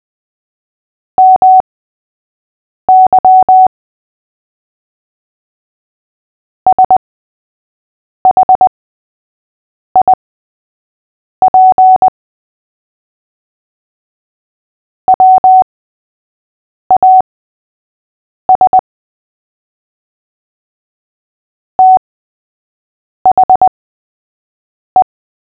Morse code, heard as a keyed tone, about 20 wpm, spaced as for 6 wpm, 750 Hz. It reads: MY SHIP WAS THE
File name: cw / fists / Short-Story-3-6wpm